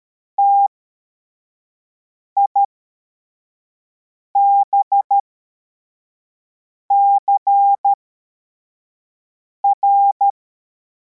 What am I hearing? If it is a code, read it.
Morse code, heard as a keyed tone, about 13 wpm, spaced as for 4 wpm, 800 Hz. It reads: TIBCR
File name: formation / cw / Text52